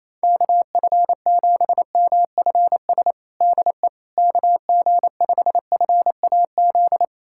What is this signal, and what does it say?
Morse code, heard as a keyed tone, about 27 wpm, 700 Hz. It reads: KF7MFS DE KG5FAZ
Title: USA Callsign DE Callsign - 6 Characters - 28wpm